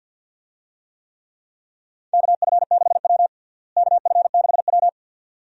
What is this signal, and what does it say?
Morse code, heard as a keyed tone, about 50 wpm, 700 Hz. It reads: KF6W KF6W
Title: USA Call Signs - 4 Characters - Repeated 2x - 50wpm